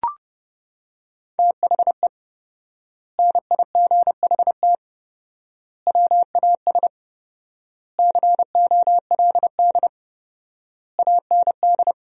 Morse code, heard as a keyed tone, about 30 wpm, 700 Hz. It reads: THE NIGHT WAS COLD AND
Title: 4x Word-Spacing - Sentences from Top 400 Words - 30wpm